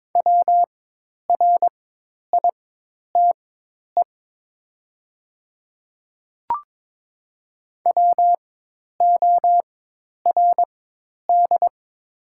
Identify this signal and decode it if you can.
Morse code, heard as a keyed tone, about 22 wpm, spaced as for 10 wpm, 700 Hz. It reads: WRITE WORD
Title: 4x Character Spacing - Top 200 Words - 22wpm